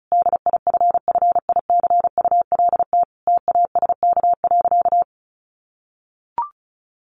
Morse code, heard as a keyed tone, about 34 wpm, 700 Hz. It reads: DIFFICULT TASK.
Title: Sentences from Top 1300 Words 35wpm